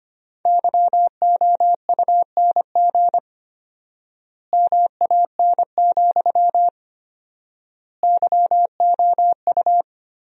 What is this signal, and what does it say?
Morse code, heard as a keyed tone, about 25 wpm, 700 Hz. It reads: YOUNG MAN, YOU
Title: 4x Word-Spacing - Sentences from Top 400 Words - 25wpm